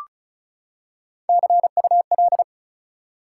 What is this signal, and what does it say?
Morse code, heard as a keyed tone, about 36 wpm, 700 Hz. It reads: CUL
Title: Top 180 QSO Elements 35wpm